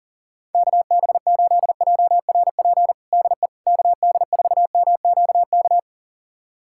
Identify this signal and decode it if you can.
Morse code, heard as a keyed tone, about 40 wpm, 700 Hz. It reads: KB8JRP DE KD4MQK
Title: USA Callsign DE Callsign - 6 Characters - 40wpm